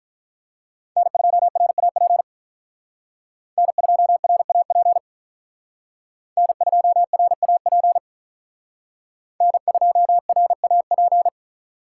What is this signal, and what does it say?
Morse code, heard as a keyed tone, about 42 wpm, 700 Hz. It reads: N2RAP N2RAP N2RAP N2RAP
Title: Speed Racing - USA Call Signs 35wpm